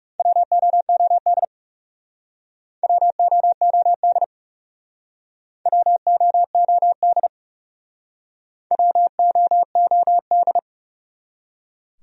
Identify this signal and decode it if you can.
Morse code, heard as a keyed tone, about 38 wpm, 700 Hz. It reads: WOOD WOOD WOOD WOOD